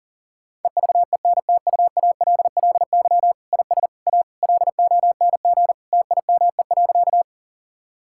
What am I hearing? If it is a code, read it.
Morse code, heard as a keyed tone, about 37 wpm, 700 Hz. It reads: EVENTUALLY IS A LONG TIME.
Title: Sentences from Top 1000 Words - Review Part 6 - 40wpm